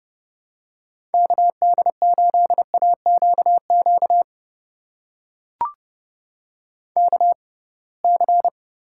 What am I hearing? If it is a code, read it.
Morse code, heard as a keyed tone, about 31 wpm, spaced as for 23 wpm, 700 Hz. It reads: KD8AQQ KC
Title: USA Call Signs - 6 Characters - 6x Character Spacing - 30wpm